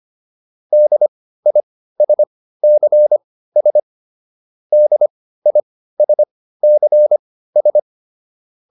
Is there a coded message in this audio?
Yes: Morse code from 0.7 to 7.8 s, heard as a keyed tone, about 25 wpm, spaced as for 15 wpm, 600 Hz. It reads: DISCS DISCS